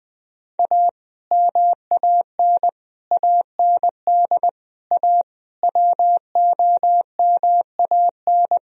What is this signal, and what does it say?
Morse code, heard as a keyed tone, about 20 wpm, 700 Hz. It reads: A MAN AND A WOMAN
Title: Speed-Racing - Sets of 5 Words 17wpm